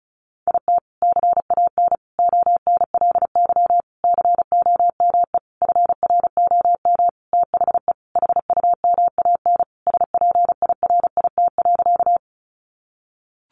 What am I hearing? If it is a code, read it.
Morse code, heard as a keyed tone, about 34 wpm, 700 Hz. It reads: IT CAN ONLY COME FROM THE HUMAN SPIRIT.
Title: Sentences from Top 1400 Words 35wpm